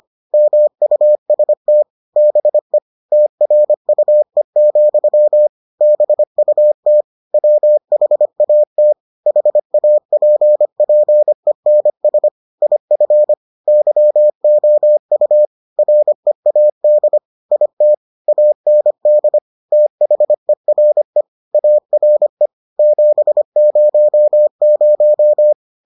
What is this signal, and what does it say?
Morse code, heard as a keyed tone, about 25 wpm, 600 Hz. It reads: MUST BE TRUE, BUT WHAT HAPPENS IF YOU READ IT AND THERE ARE 700